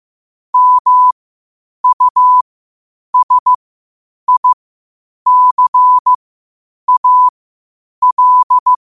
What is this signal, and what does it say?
Morse code, heard as a keyed tone, about 15 wpm, spaced as for 8 wpm, 1000 Hz. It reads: MUSICAL
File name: qotdfiles / 8wpm